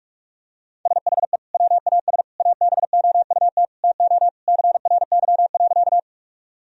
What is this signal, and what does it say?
Morse code, heard as a keyed tone, about 46 wpm, 700 Hz. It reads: SHE WAS ABOUT TO CRY.
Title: Sentences from Top 1000 Words - Review Part 2 - 45wpm